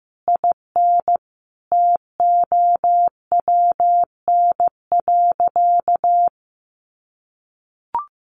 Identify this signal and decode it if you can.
Morse code, heard as a keyed tone, about 15 wpm, 700 Hz. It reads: IN TOWN.
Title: Sentences from Top 1500 Words 15wpm